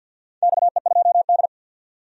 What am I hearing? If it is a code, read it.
Morse code, heard as a keyed tone, about 53 wpm, 700 Hz. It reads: XE2D
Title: DXCC-100 Call Signs - 4 Characters - 50wpm